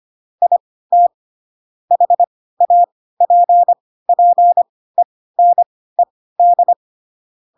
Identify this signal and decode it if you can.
Morse code, heard as a keyed tone, about 25 wpm, spaced as for 16 wpm, 700 Hz. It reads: IT HAPPENED
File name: Pinocchio / Pinocchio ch1 10 25 ik1hge